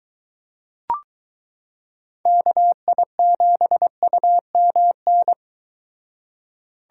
Morse code, heard as a keyed tone, about 23 wpm, 700 Hz. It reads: KI7UMN